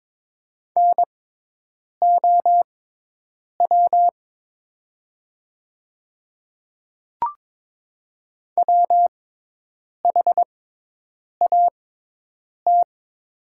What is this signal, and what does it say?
Morse code, heard as a keyed tone, about 22 wpm, spaced as for 8 wpm, 700 Hz. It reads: NOW WHAT
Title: 6x Character Spacing - Top 300 Words - 22wpm